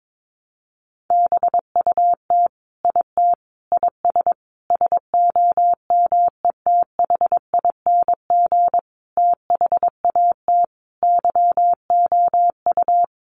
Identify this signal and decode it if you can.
Morse code, heard as a keyed tone, about 22 wpm, 700 Hz. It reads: BUT IT IS SOMETHING THAT YOU
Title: Sentences from Top 1000 Words 22wpm